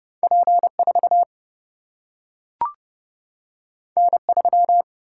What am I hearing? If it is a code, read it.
Morse code, heard as a keyed tone, about 30 wpm, 700 Hz. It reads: P 4 N 3
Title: Single-Letter Single-Number 30wpm